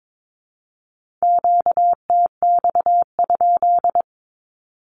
Morse code, heard as a keyed tone, about 22 wpm, 700 Hz. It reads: QTX?
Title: Q-Codes with Spoken Meaning - 22wpm